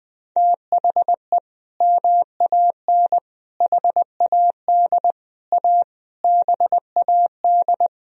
Morse code, heard as a keyed tone, about 20 wpm, 700 Hz. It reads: THE MAN HAD A BAD
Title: Sentences from Top 1000 Words - Review Part 2 - 20wpm